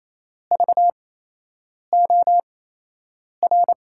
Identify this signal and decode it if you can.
Morse code, heard as a keyed tone, about 28 wpm, spaced as for 8 wpm, 700 Hz. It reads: VOR
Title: USA Call Signs - 6 Characters - 8x Character Spacing - 28wpm